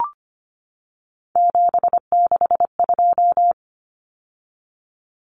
Morse code, heard as a keyed tone, about 25 wpm, 700 Hz. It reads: 762
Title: Numbers - 3-Digit 25wpm